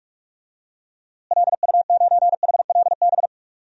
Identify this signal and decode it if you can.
Morse code, heard as a keyed tone, about 46 wpm, 700 Hz. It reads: LU9HLB
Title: DXCC-100 Call Signs - 6 Characters - Repeated 2x - 45wpm